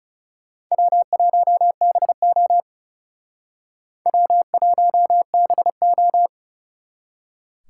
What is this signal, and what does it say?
Morse code, heard as a keyed tone, about 36 wpm, 700 Hz. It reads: W1BO W1BO
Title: Speed Racing - USA Call Signs 30wpm